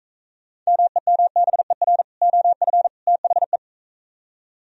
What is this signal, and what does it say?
Morse code, heard as a keyed tone, about 43 wpm, 700 Hz. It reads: MEMBER OF THE